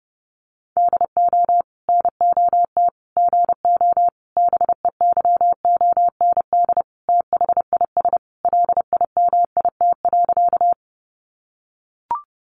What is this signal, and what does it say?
Morse code, heard as a keyed tone, about 30 wpm, 700 Hz. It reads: DO NOT GO BEYOND THIS LIMIT.
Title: Sentences from Top 1500 Words 30wpm